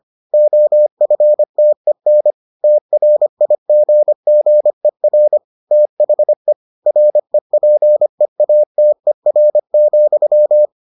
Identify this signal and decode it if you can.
Morse code, heard as a keyed tone, about 25 wpm, 600 Hz. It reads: OFTEN TRIGGER THE REPEATER,